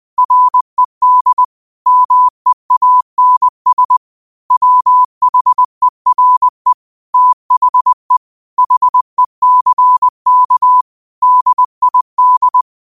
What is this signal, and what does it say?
Morse code, heard as a keyed tone, about 20 wpm, 1000 Hz. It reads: RED MEANS WHERE THE HECK DID
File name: qotdfiles / 20wpm